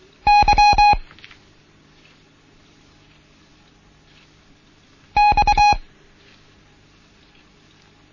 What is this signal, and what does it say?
Morse code, heard as a keyed tone, about 23 wpm, 830 Hz. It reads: Y X
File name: Morsen / Kurs A 16